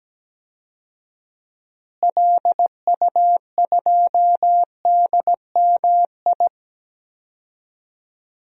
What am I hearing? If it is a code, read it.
Morse code, heard as a keyed tone, about 17 wpm, 700 Hz. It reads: LU2DMI
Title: DXCC-100 Call Signs - 6 Characters - Repeated 3x - 17wpm